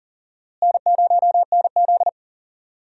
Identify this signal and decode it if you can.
Morse code, heard as a keyed tone, about 39 wpm, 700 Hz. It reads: N0NZ
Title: USA Call Signs - 4 Characters - Repeated 2x - 40wpm